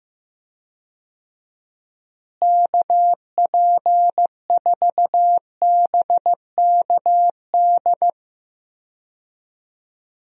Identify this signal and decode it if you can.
Morse code, heard as a keyed tone, about 16 wpm, 700 Hz. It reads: KP4BKD
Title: DXCC-100 Call Signs - 6 Characters - Repeated 3x - 15wpm